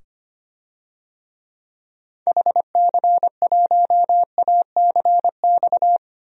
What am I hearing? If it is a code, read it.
Morse code, heard as a keyed tone, about 25 wpm, 700 Hz. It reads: HC1ACX
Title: DXCC-100 Call Signs - 6 Characters - 25wpm